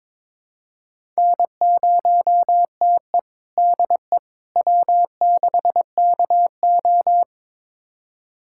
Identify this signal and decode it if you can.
Morse code, heard as a keyed tone, about 22 wpm, 700 Hz. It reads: N0TE DE W6KO
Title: USA Callsign DE Callsign - 4 Characters - 22wpm